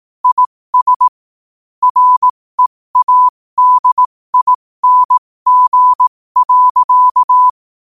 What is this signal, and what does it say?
Morse code, heard as a keyed tone, about 18 wpm, spaced as for 15 wpm, 1000 Hz. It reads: IS READING.
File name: qotdfiles / 15wpm